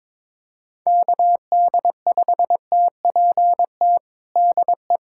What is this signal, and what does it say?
Morse code, heard as a keyed tone, about 22 wpm, 700 Hz. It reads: KD5TPT DE
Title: USA Callsign DE Callsign 22wpm